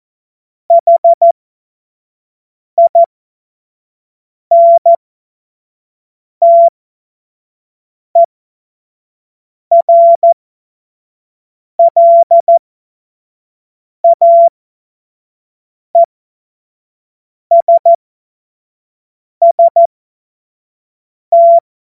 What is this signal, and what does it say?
Morse code, heard as a keyed tone, about 14 wpm, spaced as for 5 wpm, 680 Hz. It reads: HINTERLAESST
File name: Witz-07-Oktober-2025-25-F70